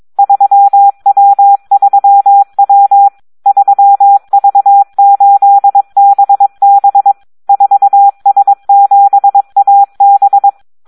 Morse code, heard as a keyed tone, about 22 wpm, 800 Hz. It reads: 3W3W 3V8BB 4S7AB
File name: Audio practica CW / SD 1 22